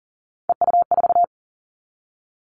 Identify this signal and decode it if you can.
Morse code, heard as a keyed tone, about 38 wpm, 700 Hz. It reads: EU4
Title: Callsign Prefixes for Top 50 DXCC - 40wpm